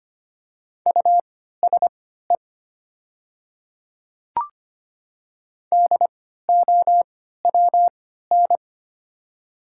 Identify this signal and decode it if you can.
Morse code, heard as a keyed tone, about 25 wpm, spaced as for 14 wpm, 700 Hz. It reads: USE DOWN
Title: 3x Character Spacing - Top 200 Words - 25wpm